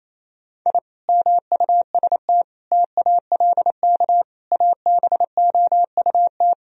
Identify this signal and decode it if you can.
Morse code, heard as a keyed tone, about 28 wpm, 700 Hz. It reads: I MUST TALK ABOUT
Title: Sentences from Top 1000 Words - Review Part 4 - 28wpm